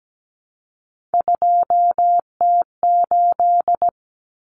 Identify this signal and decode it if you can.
Morse code, heard as a keyed tone, about 17 wpm, 700 Hz. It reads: 2T8
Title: Common Cut Numbers - 3-Digit 17wpm